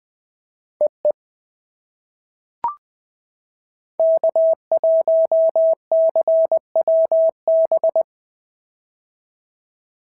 Morse code, heard as a keyed tone, about 19 wpm, 690 Hz. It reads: EE K1CWB
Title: POTA QSO Practice - 20wpm